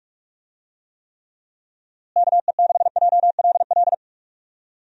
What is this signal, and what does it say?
Morse code, heard as a keyed tone, about 43 wpm, 700 Hz. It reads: KE6JLL
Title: USA Call Signs - 6 Characters - Repeated 3x - 45wpm